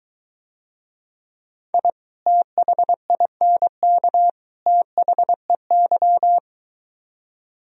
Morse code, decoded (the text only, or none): I THINK THEY